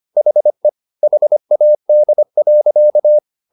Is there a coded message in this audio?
Yes: Morse code from 0.2 to 3.2 s, heard as a keyed tone, about 25 wpm, 600 Hz. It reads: HE HAD.